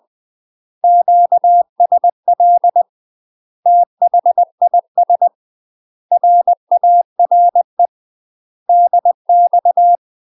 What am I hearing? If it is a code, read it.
Morse code, heard as a keyed tone, about 20 wpm, 700 Hz. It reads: QSL THIS RARE DX
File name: UK MASTER QSOs TRAINING / 20WPM / UK MASTER QSOs 0032